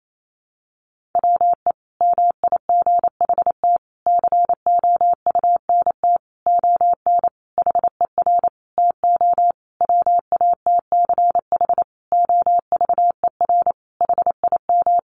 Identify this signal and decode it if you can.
Morse code, heard as a keyed tone, about 28 wpm, 700 Hz. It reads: WE MIGHT COUNT ON HER TO WATCH OVER HIM
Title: Sentences from Top 1500 Words 28wpm